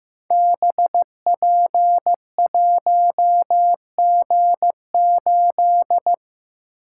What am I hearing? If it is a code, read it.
Morse code, heard as a keyed tone, about 15 wpm, 700 Hz. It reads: BP1G8
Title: Code Groups - 5 Letter:Numbers - 15wpm